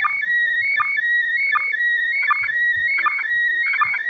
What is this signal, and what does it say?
SSTV (slow-scan television), PD 180.
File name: spazio / ISS 145 800MHz SSTV 3